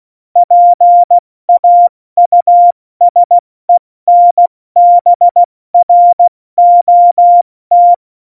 Morse code, heard as a keyed tone, about 16 wpm, 700 Hz. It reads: PAUSENBROT
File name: Morsekurs / Witze / Witz 24. Oktober 17, Tempo 70